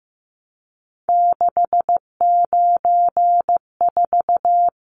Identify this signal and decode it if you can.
Morse code, heard as a keyed tone, about 15 wpm, 700 Hz. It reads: 694
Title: Numbers - 3-Digit 15wpm